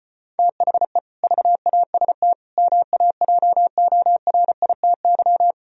Morse code, heard as a keyed tone, about 33 wpm, 700 Hz. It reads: THE VAST MAJORITY